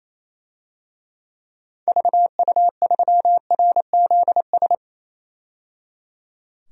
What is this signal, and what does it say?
Morse code, heard as a keyed tone, about 28 wpm, 700 Hz. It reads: VU3RZS
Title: DXCC-100 Call Signs - 6 Characters - 28wpm